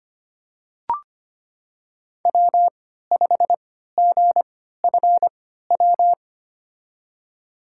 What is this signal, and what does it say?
Morse code, heard as a keyed tone, about 25 wpm, spaced as for 14 wpm, 700 Hz. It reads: W5GFW
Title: USA Call Signs - 5 Characters - 3x Character Spacing - 25wpm